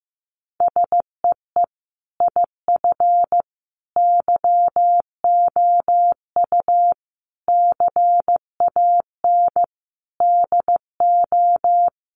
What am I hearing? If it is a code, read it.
Morse code, heard as a keyed tone, about 14 wpm, 700 Hz. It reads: SEE IF YOU CAN DO
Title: Sentences from Top 1000 Words 15wpm